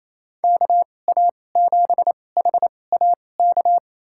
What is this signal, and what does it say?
Morse code, heard as a keyed tone, about 27 wpm, spaced as for 20 wpm, 700 Hz. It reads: KA7HAK
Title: USA Call Signs - 6 Characters - 2x Character Spacing - 28wpm